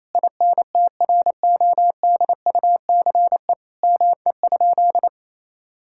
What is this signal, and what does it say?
Morse code, heard as a keyed tone, about 28 wpm, 700 Hz. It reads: INTRODUCE ME?
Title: Sentences from Top 1600 Words 28wpm